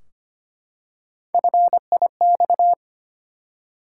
Morse code, heard as a keyed tone, about 25 wpm, 700 Hz. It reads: FIX
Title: Three Letter Words - 25wpm